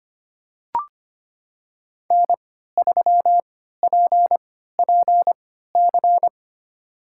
Morse code, heard as a keyed tone, about 25 wpm, spaced as for 14 wpm, 700 Hz. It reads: N3PPC
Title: USA Call Signs - 3x Character Spacing - 25wpm